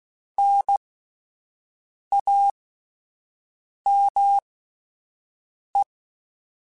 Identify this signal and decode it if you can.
Morse code, heard as a keyed tone, about 16 wpm, spaced as for 5 wpm, 800 Hz. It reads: NAME
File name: (spaced) / QSOs Spaced-mp3s / kc7haa @10 sp